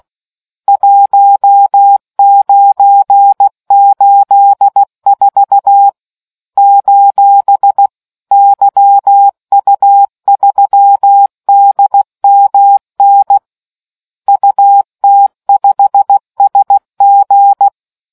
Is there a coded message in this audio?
Yes: Morse code from 0.7 to 17.7 s, heard as a keyed tone, about 16 wpm, 800 Hz. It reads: 1984 : YU3DMN UT5SG